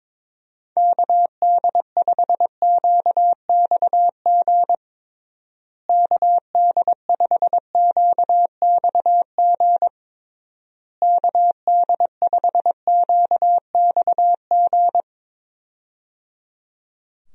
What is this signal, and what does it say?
Morse code, heard as a keyed tone, about 22 wpm, 700 Hz. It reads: KD5QXG KD5QXG KD5QXG
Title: USA Call Signs - 6 Characters - Repeated 3x - 22wpm